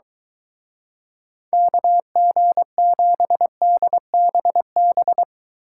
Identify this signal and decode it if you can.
Morse code, heard as a keyed tone, about 23 wpm, 700 Hz. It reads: KG7DBB